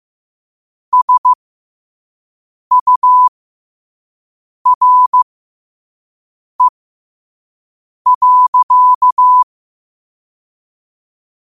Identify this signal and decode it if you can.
Morse code, heard as a keyed tone, about 15 wpm, spaced as for 5 wpm, 1000 Hz. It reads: SURE.
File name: qotdfiles / 5wpm